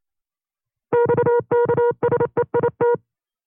Morse code, heard as a keyed tone, about 28 wpm, 480 Hz. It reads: XKSEIT
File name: FreeFiles / CW-practice / 25WPM-side2